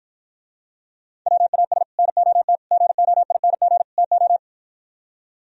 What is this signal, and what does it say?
Morse code, heard as a keyed tone, about 54 wpm, 700 Hz. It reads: WAS NOT GOING TO